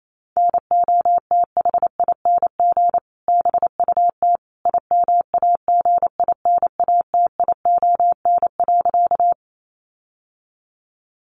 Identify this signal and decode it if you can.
Morse code, heard as a keyed tone, about 28 wpm, 700 Hz. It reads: NOTHING BUT IMAGINATION.
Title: Sentences from Top 1300 Words 28wpm